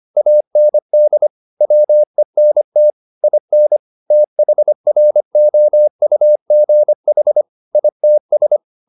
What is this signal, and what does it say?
Morse code, heard as a keyed tone, about 25 wpm, 600 Hz. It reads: AND WENT IN THROUGH ITS
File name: the old man and the sea part 1 25wpm